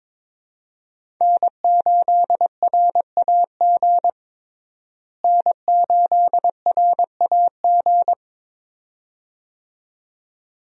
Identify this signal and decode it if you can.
Morse code, heard as a keyed tone, about 22 wpm, 700 Hz. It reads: N8RAG N8RAG
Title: USA Call Signs - 5 Characters - Repeated 2x - 22wpm